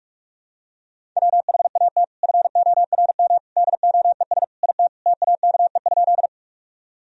Morse code, heard as a keyed tone, about 42 wpm, 700 Hz. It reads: WHAT FORM DOES IT TAKE?
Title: Sentences from Top 1000 Words - Review Part 1 - 45wpm